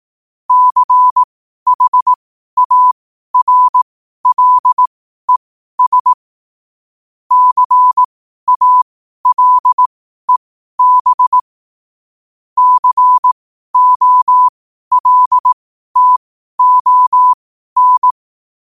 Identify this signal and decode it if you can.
Morse code, heard as a keyed tone, about 18 wpm, spaced as for 12 wpm, 1000 Hz. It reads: CHARLES CALEB COLTON
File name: qotdfiles / 12wpm